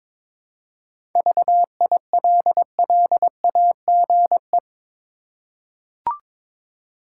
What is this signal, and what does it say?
Morse code, heard as a keyed tone, about 22 wpm, 700 Hz. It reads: VILLAGE
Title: Seven Letter Words - 22wpm